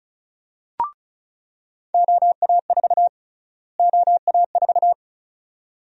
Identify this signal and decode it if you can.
Morse code, heard as a keyed tone, about 34 wpm, 700 Hz. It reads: OA4 OA4
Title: USA Call Sign Prefixes - Repeated 2x - 35wpm